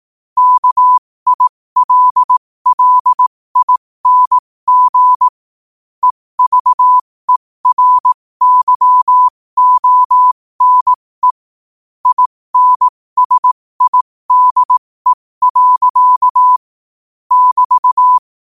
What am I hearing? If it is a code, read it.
Morse code, heard as a keyed tone, about 18 wpm, spaced as for 15 wpm, 1000 Hz. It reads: KILLING EVERYONE INSIDE. <BT>
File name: qotdfiles / 15wpm